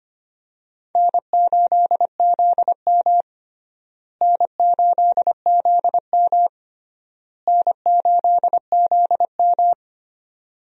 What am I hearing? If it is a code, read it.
Morse code, heard as a keyed tone, about 25 wpm, 700 Hz. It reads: N8ZM N8ZM N8ZM
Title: USA Call Signs - 4 Characters - Repeated 3x - 25wpm